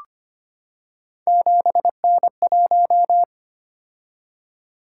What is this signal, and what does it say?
Morse code, heard as a keyed tone, about 25 wpm, 700 Hz. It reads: 7N1
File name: Three Letter-Number Combinations - 25wpm